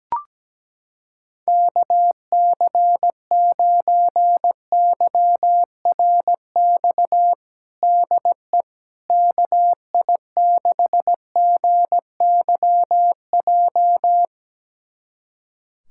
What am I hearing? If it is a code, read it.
Morse code, heard as a keyed tone, about 17 wpm, 700 Hz. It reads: KC9YRX DE KI6GYJ
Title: USA Callsign DE Callsign 17wpm